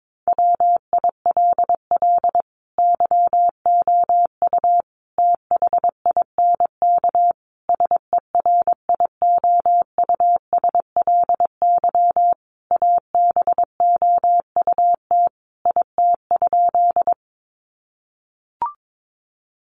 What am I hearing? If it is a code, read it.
Morse code, heard as a keyed tone, about 22 wpm, 700 Hz. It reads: WILL YOU THINK SERIOUSLY ABOUT IT?
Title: Sentences from Top 1200 Words 22wpm